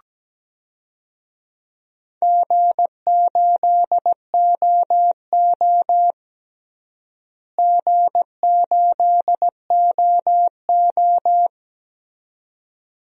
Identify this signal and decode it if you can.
Morse code, heard as a keyed tone, about 17 wpm, 700 Hz. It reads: G8OO G8OO
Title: DXCC-100 Call Signs - 4 Characters - Repeated 2x - 17wpm